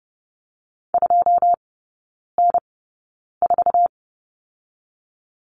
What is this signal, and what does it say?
Morse code, heard as a keyed tone, about 30 wpm, spaced as for 9 wpm, 700 Hz. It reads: 2N4
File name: Lesson 18e - Letter m - Three Characters 30wpm